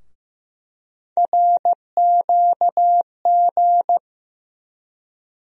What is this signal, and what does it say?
Morse code, heard as a keyed tone, about 15 wpm, 700 Hz. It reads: RQG
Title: USA Call Sign Postfixes - 3 Letters 15wpm